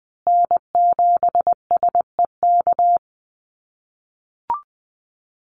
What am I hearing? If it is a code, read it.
Morse code, heard as a keyed tone, about 20 wpm, 700 Hz. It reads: N7SEK
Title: USA Call Signs Repeated 2x 20wpm